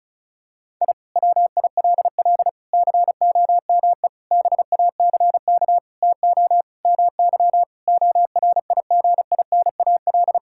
Morse code, heard as a keyed tone, about 36 wpm, 690 Hz. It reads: I WILL COME BACK TO MY ORIGINAL
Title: Sentences from Top 1000 Words - Review Part 6 - 35wpm